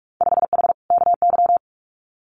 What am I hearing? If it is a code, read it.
Morse code, heard as a keyed tone, about 46 wpm, 700 Hz. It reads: 5H KY
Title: Winter Field Day Exchanges Encore 45wpm